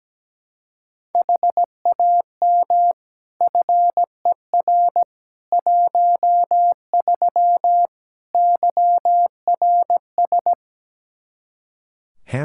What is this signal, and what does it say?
Morse code, heard as a keyed tone, about 17 wpm, 700 Hz. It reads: HAM FER 13 YRS
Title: K1USN SST QSO Practice 17wpm